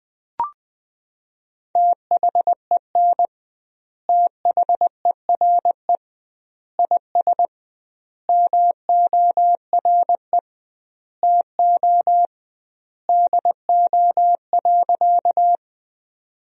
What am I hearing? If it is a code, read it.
Morse code, heard as a keyed tone, about 21 wpm, 700 Hz. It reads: THEN THERE IS MORE TO DO.
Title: 2x Word-Spacing - Sentences from Top 100 Words - 20wpm